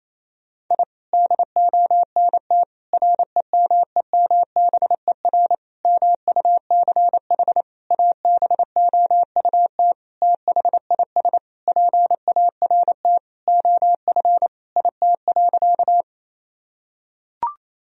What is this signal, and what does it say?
Morse code, heard as a keyed tone, about 28 wpm, 700 Hz. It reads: I DONT REMEMBER MUCH ABOUT THIS PART OF IT.
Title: Sentences from Top 1000 Words - Review Part 1 - 28wpm